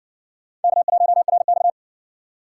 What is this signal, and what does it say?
Morse code, heard as a keyed tone, about 60 wpm, 700 Hz. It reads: K1RX